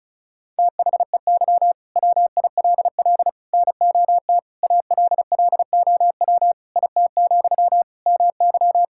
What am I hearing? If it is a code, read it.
Morse code, heard as a keyed tone, about 36 wpm, 700 Hz. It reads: THEY WILL NOT ALLOW IT, MY
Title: Sentences from Top 1000 Words - Review Part 5 - 35wpm